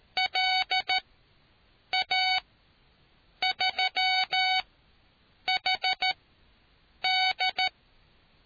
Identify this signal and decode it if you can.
Morse code, heard as a keyed tone, about 13 wpm, spaced as for 7 wpm, 750 Hz. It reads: LA3HD